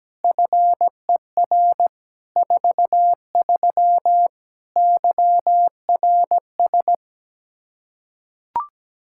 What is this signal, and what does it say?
Morse code, heard as a keyed tone, about 17 wpm, 700 Hz. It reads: FER 43 YRS
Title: Practice QSO - Exchange 3 - More Info - 17wpm